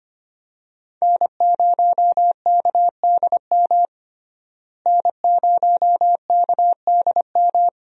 Morse code, heard as a keyed tone, about 25 wpm, 700 Hz. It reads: N0KDM N0KDM
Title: USA Call Signs - 5 Characters - Repeated 2x - 25wpm